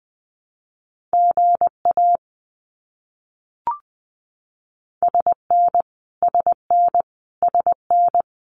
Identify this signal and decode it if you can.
Morse code, heard as a keyed tone, about 20 wpm, 700 Hz. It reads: GA SN SN SN
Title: Abbreviations Repeated 3x - 20wpm